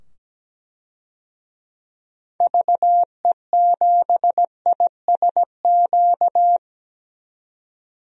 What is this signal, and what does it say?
Morse code, heard as a keyed tone, about 17 wpm, 700 Hz. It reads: VE7ISQ
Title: DXCC-100 Call Signs - 6 Characters - 17wpm